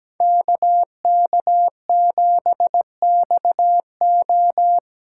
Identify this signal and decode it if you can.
Morse code, heard as a keyed tone, about 20 wpm, 700 Hz. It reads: KK7XO
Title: Speed Racing - USA 2x2 Call Signs 17wpm